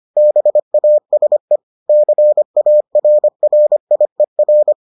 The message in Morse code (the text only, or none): BASE CARRIER